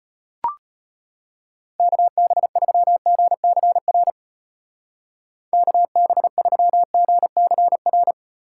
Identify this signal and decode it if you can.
Morse code, heard as a keyed tone, about 32 wpm, 700 Hz. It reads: KB3GCR KB3GCR